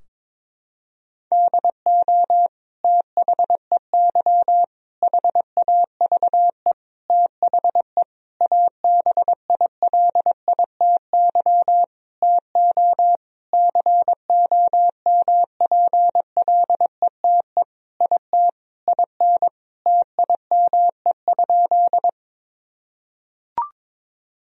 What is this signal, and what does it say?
Morse code, heard as a keyed tone, about 21 wpm, 700 Hz. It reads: DO THEY HAVE THE ABILITY TO COMPLETE IT IN TIME?
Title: Sentences from Top 1000 Words - Review Part 3 - 22wpm